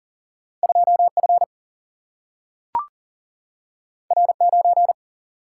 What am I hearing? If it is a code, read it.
Morse code, heard as a keyed tone, about 40 wpm, 700 Hz. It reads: 2 F R 9
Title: Single-Letter Single-Number 40wpm